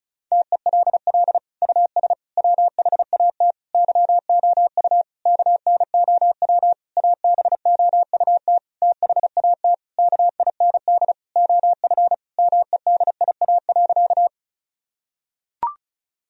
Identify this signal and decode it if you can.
Morse code, heard as a keyed tone, about 36 wpm, 700 Hz. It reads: TELL US WHAT YOU KNOW ABOUT THAT KIND OF MEDIA.
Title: Sentences from Top 1000 Words - Review Part 4 - 35wpm